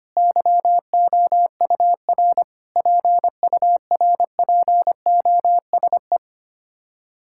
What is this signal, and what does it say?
Morse code, heard as a keyed tone, about 25 wpm, 700 Hz. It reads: YOUR PURPOSE
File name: Sets of 9 Words - 25wpm